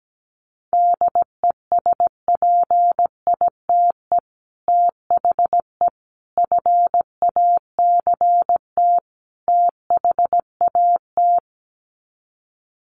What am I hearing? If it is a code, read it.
Morse code, heard as a keyed tone, about 17 wpm, 700 Hz. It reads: DESPITE THE FACT THAT
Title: Sets of 4 Words 17wpm